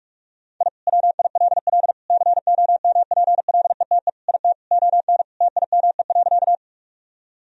Morse code, heard as a keyed tone, about 46 wpm, 700 Hz. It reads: I WILL COMPLETE IT ON TIME.
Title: Sentences from Top 1000 Words - Review Part 2 - 45wpm